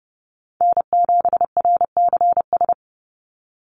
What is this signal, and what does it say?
Morse code, heard as a keyed tone, about 30 wpm, 700 Hz. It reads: N7RCS
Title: Call Signs Repeated 3x - 30wpm